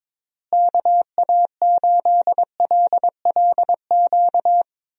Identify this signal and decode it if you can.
Morse code, heard as a keyed tone, about 22 wpm, 700 Hz. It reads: KA8LLQ
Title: USA Call Signs - 6 Characters - Repeated 3x - 22wpm